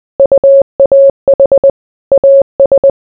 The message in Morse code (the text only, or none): UAH AS